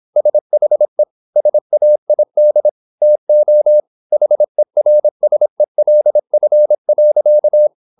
Morse code, heard as a keyed tone, about 26 wpm, 600 Hz. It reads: SHE SAID TO HERSELF.